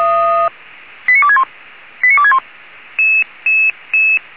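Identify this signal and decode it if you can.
Five-tone selective call (ZVEI), heard as quick runs of five tones, tones of about 70 ms. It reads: 89261 89261
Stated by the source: Five-tone ZVEI selective call received on SDR.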